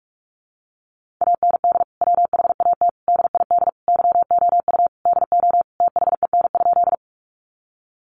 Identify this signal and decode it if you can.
Morse code, heard as a keyed tone, about 44 wpm, 700 Hz. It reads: AND WHAT DID YOU DO THEN?
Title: Sentences from Top 100 Words 45wpm